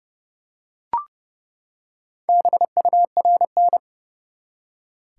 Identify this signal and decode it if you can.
Morse code, heard as a keyed tone, about 30 wpm, 700 Hz. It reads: BURN
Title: Four Letter Words - 30wpm